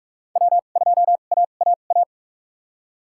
Morse code, heard as a keyed tone, about 45 wpm, spaced as for 32 wpm, 700 Hz. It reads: W2AAA
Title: USA Call Signs - 2x Character Spacing - 45wpm